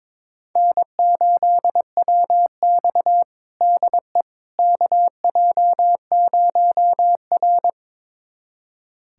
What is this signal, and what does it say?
Morse code, heard as a keyed tone, about 22 wpm, 700 Hz. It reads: N8WX DE KJ0R
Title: USA Callsign DE Callsign - 4 Characters - 22wpm